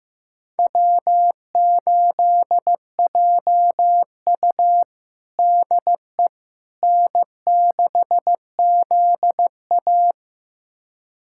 Morse code, heard as a keyed tone, about 15 wpm, 700 Hz. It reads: W8JU DE N6ZA
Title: USA Callsign DE Callsign - 4 Characters - 15wpm